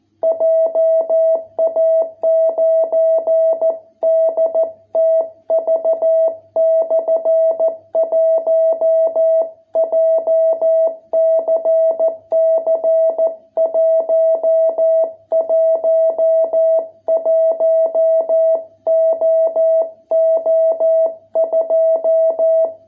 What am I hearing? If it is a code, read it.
Morse code, heard as a keyed tone, about 14 wpm, 660 Hz. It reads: JA9DTV/1 JCC 111OO2